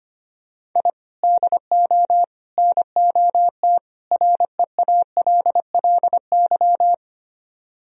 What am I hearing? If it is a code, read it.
Morse code, heard as a keyed tone, about 24 wpm, 700 Hz. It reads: I DO NOT REALLY
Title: Speed-Racing - Sets of 4 Words 25wpm